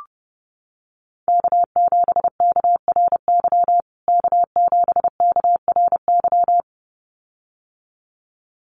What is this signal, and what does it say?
Morse code, heard as a keyed tone, about 30 wpm, 700 Hz. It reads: K7KRY K7KRY
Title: USA Call Signs Repeated 2x 30wpm